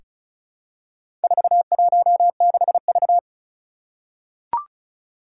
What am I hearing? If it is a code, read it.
Morse code, heard as a keyed tone, about 36 wpm, 690 Hz. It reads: 416V
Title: Four Challenging Letter-Number Combinations 35wpm